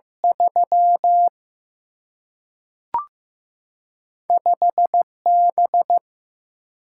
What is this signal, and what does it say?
Morse code, heard as a keyed tone, about 15 wpm, 700 Hz. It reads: 3 5 B
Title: Single-Letter Single-Number 15wpm